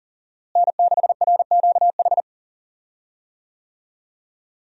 Morse code, heard as a keyed tone, about 37 wpm, 700 Hz. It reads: N6RQH
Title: USA Call Signs - 5 Characters - Repeated 3x - 40wpm